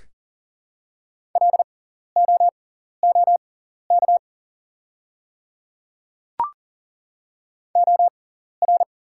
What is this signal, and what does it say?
Morse code, heard as a keyed tone, about 40 wpm, spaced as for 14 wpm, 700 Hz. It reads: LOOK OR
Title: 6x Character Spacing - Top 100 Words - 40wpm